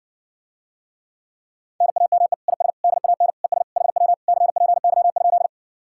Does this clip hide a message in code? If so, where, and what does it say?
Yes: Morse code from 1.8 to 5.5 s, heard as a keyed tone, about 60 wpm, 700 Hz. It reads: NAME IS DAN ES HW CPY?